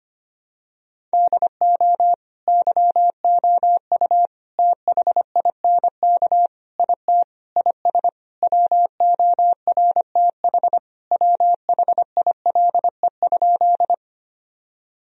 Morse code, heard as a keyed tone, about 25 wpm, 700 Hz. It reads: DO YOU THINK IT IS WORTH WHILE?
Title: Sentences from Top 1000 Words - Review Part 1 - 25wpm